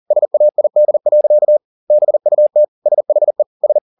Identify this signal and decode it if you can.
Morse code, heard as a keyed tone, about 40 wpm, 600 Hz. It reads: SAID. BUT SHE S
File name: the old man and the sea 2 40wpm